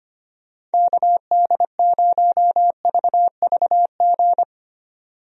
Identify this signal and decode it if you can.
Morse code, heard as a keyed tone, about 25 wpm, 700 Hz. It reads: KD0VVG
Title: USA Call Signs - 6 Characters - 25wpm